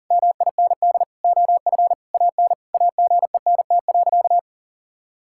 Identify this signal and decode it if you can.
Morse code, heard as a keyed tone, about 40 wpm, 700 Hz. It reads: MIND OF AN AGENT.
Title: Sentences from Top 1600 Words 40wpm